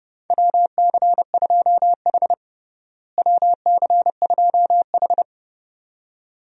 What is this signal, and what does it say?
Morse code, heard as a keyed tone, about 32 wpm, 700 Hz. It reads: WC2H WC2H
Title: USA Call Signs - 4 Characters - Repeated 2x - 30wpm